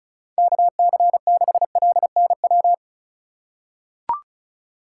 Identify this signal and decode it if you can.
Morse code, heard as a keyed tone, about 36 wpm, 700 Hz. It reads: KC6LNW
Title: USA Call Signs - 6 Characters - 4x Character Spacing - 35wpm